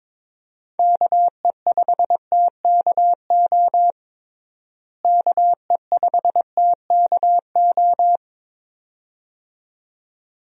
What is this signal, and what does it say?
Morse code, heard as a keyed tone, about 22 wpm, 700 Hz. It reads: KE5TKO KE5TKO
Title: USA Call Signs - 6 Characters - Repeated 2x - 22wpm